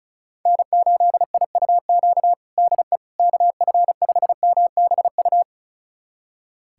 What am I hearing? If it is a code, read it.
Morse code, heard as a keyed tone, about 36 wpm, 700 Hz. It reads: N8IUQ DE KF5MBU
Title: USA Callsign DE Callsign 35wpm